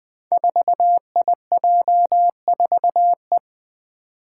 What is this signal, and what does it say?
Morse code, heard as a keyed tone, about 20 wpm, 700 Hz. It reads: 4IJ4E
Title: Five Challenging Letter-Number Combinations 20wpm